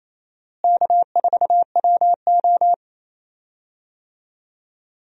Morse code, heard as a keyed tone, about 28 wpm, 700 Hz. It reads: K4WO
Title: USA Call Signs - 4 Characters - Repeated 2x - 28wpm